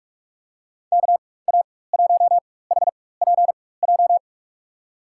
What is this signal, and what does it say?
Morse code, heard as a keyed tone, about 42 wpm, spaced as for 21 wpm, 700 Hz. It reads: KA1HPJ
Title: USA Call Signs - 6 Characters - 4x Character Spacing - 45wpm